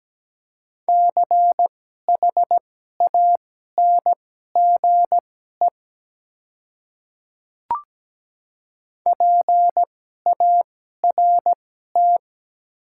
Morse code, heard as a keyed tone, about 17 wpm, spaced as for 12 wpm, 700 Hz. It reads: CHANGE PART
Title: 2x Character Spacing - Top 300 Words - 17wpm